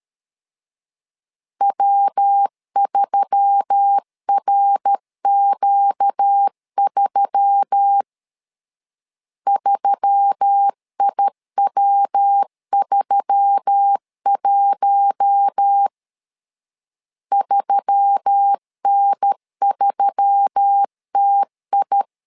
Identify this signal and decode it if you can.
Morse code, heard as a keyed tone, about 13 wpm, 800 Hz. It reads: W3RQ3 3IW31 3N3TI
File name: LEZIONI / 4 Q1S3 12wpm 2023